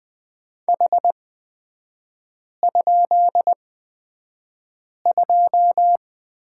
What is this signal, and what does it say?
Morse code, heard as a keyed tone, about 20 wpm, spaced as for 5 wpm, 700 Hz. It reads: H?2
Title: 250 Lesson 40e - BK - Three Characters 20wpm